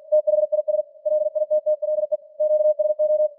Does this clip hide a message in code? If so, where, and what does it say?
Yes: Morse code from 0.1 to 3.3 s, heard as a keyed tone, about 46 wpm, 610 Hz. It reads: THIS LITTLE QSO